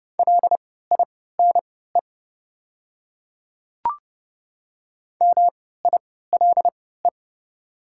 Morse code, heard as a keyed tone, about 30 wpm, spaced as for 17 wpm, 700 Hz. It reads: LINE MILE
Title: 3x Character Spacing - Top 300 Words - 30wpm